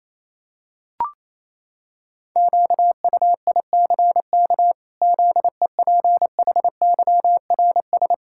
Morse code, heard as a keyed tone, about 28 wpm, 700 Hz. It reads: QUICK ZEPHYRS
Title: Pangram Challenge v2 - 28wpm